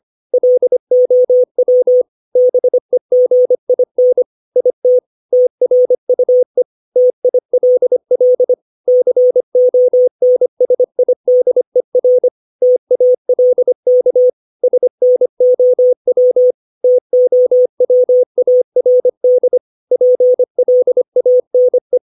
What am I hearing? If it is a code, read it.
Morse code, heard as a keyed tone, about 25 wpm, 500 Hz. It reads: LOW BEGIN IT TRUE TILL CONSIDER TALK SNOW TOWARD PLANE
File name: 632 words / 632 Words 25WPM